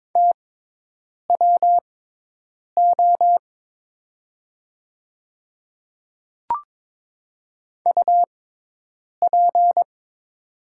Morse code, heard as a keyed tone, about 22 wpm, spaced as for 8 wpm, 700 Hz. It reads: TWO UP
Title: 6x Character Spacing - Top 200 Words - 22wpm